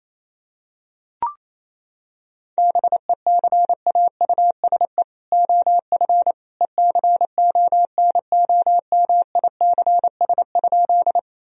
Morse code, heard as a keyed tone, about 28 wpm, 700 Hz. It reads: BECAUSE OF ECONOMICS?
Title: Sentences from Top 1000 Words - Review Part 3 - 28wpm